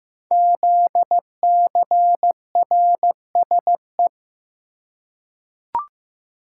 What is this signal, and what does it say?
Morse code, heard as a keyed tone, about 15 wpm, 700 Hz. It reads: ZCRSE
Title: Code Groups - 5 Letters - 15wpm